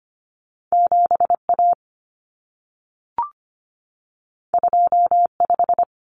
Morse code, heard as a keyed tone, about 25 wpm, 700 Hz. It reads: 7A 25
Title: Common Cut Numbers - 2-Digit 25wpm